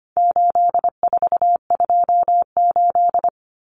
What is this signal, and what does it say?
Morse code, heard as a keyed tone, about 25 wpm, 700 Hz. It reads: 8428
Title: Numbers - 4-Digit 25wpm